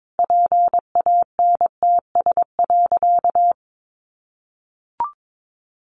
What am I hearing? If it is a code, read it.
Morse code, heard as a keyed tone, about 22 wpm, 700 Hz. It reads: PANTS.
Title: A Sound Of Thunder 22wpm